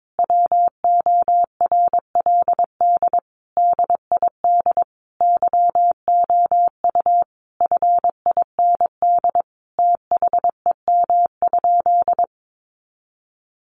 Morse code, heard as a keyed tone, about 22 wpm, 700 Hz. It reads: WORLD DID YOU FIND THEM?
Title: Sentences from Top 200 Words 22wpm